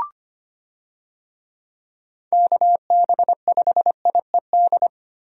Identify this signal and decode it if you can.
Morse code, heard as a keyed tone, about 25 wpm, 700 Hz. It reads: KB5IED
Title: USA Call Signs - 6 Characters - Repeated 2x - 25wpm